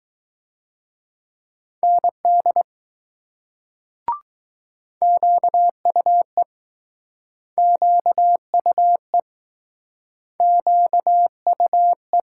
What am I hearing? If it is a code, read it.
Morse code, heard as a keyed tone, about 20 wpm, 700 Hz. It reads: ND QUE QUE QUE